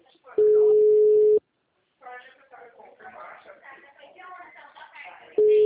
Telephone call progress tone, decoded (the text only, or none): ringback tone